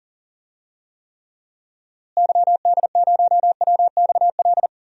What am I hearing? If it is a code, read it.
Morse code, heard as a keyed tone, about 37 wpm, 700 Hz. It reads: YD0WXL
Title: DXCC-100 Call Signs - 6 Characters - Repeated 2x - 40wpm